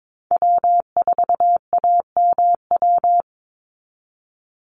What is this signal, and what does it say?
Morse code, heard as a keyed tone, about 22 wpm, 700 Hz. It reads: W4AMW
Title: NA SOTA Chaser Callsigns 22wpm